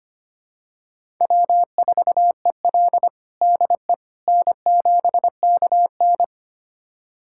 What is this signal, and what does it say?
Morse code, heard as a keyed tone, about 25 wpm, 700 Hz. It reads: W4EL DE N7KN
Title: USA Callsign DE Callsign - 4 Characters - 25wpm